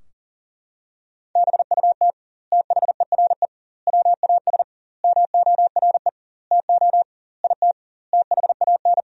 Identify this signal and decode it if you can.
Morse code, heard as a keyed tone, about 40 wpm, 700 Hz. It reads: BUT THERE WAS MORE TO IT THAN
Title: 2x Word-Spacing - Sentences from Top 100 Words - 40wpm